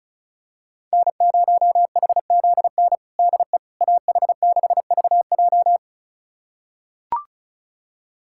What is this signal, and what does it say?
Morse code, heard as a keyed tone, about 36 wpm, 690 Hz. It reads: N0HZN DE AH6VJ
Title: USA Callsign DE Callsign - 5 Characters - 35wpm